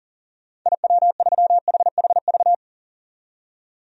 Morse code, heard as a keyed tone, about 40 wpm, 700 Hz. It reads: IW3HHV
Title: DXCC-100 Call Signs - 6 Characters - 40wpm